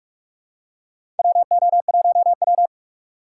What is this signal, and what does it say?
Morse code, heard as a keyed tone, about 45 wpm, 700 Hz. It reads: WO1W
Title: USA Call Signs - 4 Characters - 4x Character Spacing - 45wpm